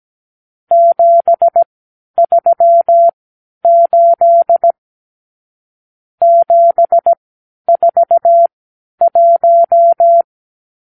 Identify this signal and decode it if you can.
Morse code, heard as a keyed tone, about 17 wpm, spaced as for 10 wpm, 680 Hz. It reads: 738 741
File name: k7qo / Lesson 54 (9)